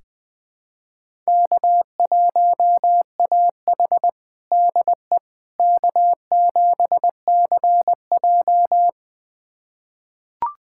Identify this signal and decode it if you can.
Morse code, heard as a keyed tone, about 20 wpm, 700 Hz. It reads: K1AH DE K7CJ
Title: USA Callsign DE Callsign - 4 Characters - 20wpm